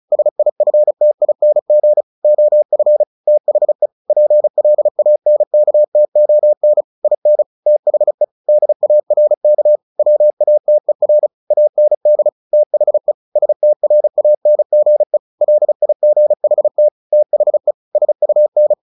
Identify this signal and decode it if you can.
Morse code, heard as a keyed tone, about 34 wpm, 600 Hz. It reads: SIFTING OF THE PLANKTON IN THE DARK WATER AND THE STRANGE LIGHT THE SUN
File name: the old man and the sea / the old man and the sea 1 35wpm